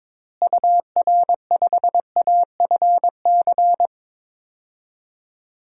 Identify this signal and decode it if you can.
Morse code, heard as a keyed tone, about 22 wpm, 700 Hz. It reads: UR5AFC
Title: DXCC-100 Call Signs - 6 Characters - Repeated 2x - 22wpm